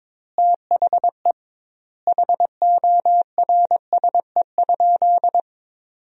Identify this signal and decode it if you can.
Morse code, heard as a keyed tone, about 22 wpm, 700 Hz. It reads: THE HORSE?
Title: 2x Word-Spacing - Sentences from Top 500 Words - 22wpm